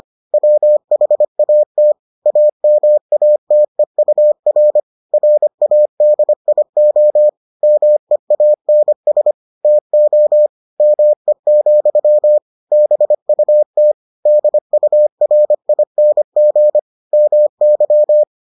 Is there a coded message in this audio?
Yes: Morse code from 0.3 to 18.2 s, heard as a keyed tone, about 25 wpm, 600 Hz. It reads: WHAT AMATEUR RADIO MEANS TO ME, BUT DURING MY